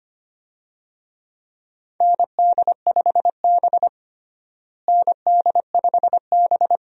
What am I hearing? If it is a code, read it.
Morse code, heard as a keyed tone, about 25 wpm, 700 Hz. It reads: ND5B ND5B
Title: USA Call Signs - 4 Characters - Repeated 3x - 25wpm